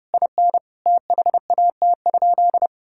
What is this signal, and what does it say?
Morse code, heard as a keyed tone, about 30 wpm, 700 Hz. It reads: IN THAT?
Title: Sentences from Top 1000 Words - Review Part 4 - 30wpm